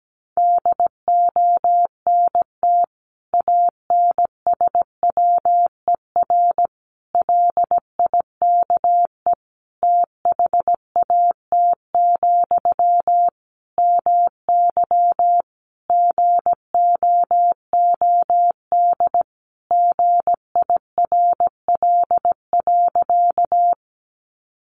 Morse code, heard as a keyed tone, about 17 wpm, 700 Hz. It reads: DONT ANSWER LIKE THAT, MY GOOD GIRL.
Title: Sentences from Top 400 Words 17wpm